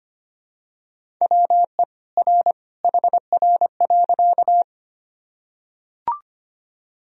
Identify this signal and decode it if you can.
Morse code, heard as a keyed tone, about 25 wpm, 700 Hz. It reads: WE R HR.
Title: Sentences from Top 200 Words and Abbreviations 25wpm